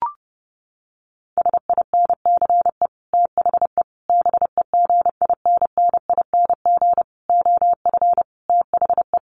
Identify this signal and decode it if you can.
Morse code, heard as a keyed tone, about 30 wpm, 700 Hz. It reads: SINCE THE BEGINNING OF THE
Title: Sets of 5 Words Encore2 30wpm